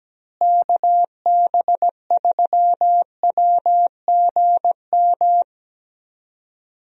Morse code, heard as a keyed tone, about 17 wpm, 700 Hz. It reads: KB3WGM
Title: USA Call Signs - 6 Characters - Repeated 3x - 17wpm